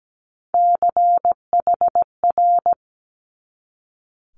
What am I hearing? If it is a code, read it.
Morse code, heard as a keyed tone, about 17 wpm, 700 Hz. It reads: CHR
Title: Multi-letter Phonemes 17wpm